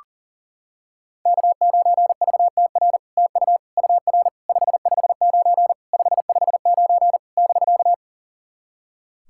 Morse code, heard as a keyed tone, about 44 wpm, 690 Hz. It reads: K9VTR TU UR 559 559 <BK>
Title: SOTA QSO Practice 40wpm